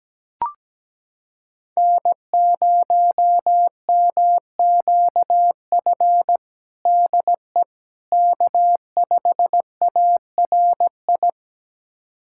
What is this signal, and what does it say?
Morse code, heard as a keyed tone, about 17 wpm, 700 Hz. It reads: N0MQF DE K5ARI
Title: USA Callsign DE Callsign 17wpm